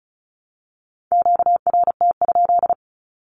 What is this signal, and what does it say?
Morse code, heard as a keyed tone, about 34 wpm, 700 Hz. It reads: QRT?
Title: Q-Codes Repeated 3x 35wpm